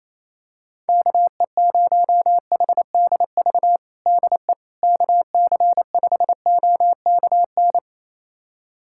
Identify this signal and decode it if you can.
Morse code, heard as a keyed tone, about 28 wpm, 700 Hz. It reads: KE0HDV DE KC5OKN
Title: USA Callsign DE Callsign 28wpm